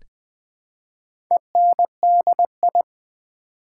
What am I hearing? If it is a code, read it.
Morse code, heard as a keyed tone, about 20 wpm, 700 Hz. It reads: ENDI
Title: Build A Word - 2 Letters At A Time - 6-Letter Words - 20wpm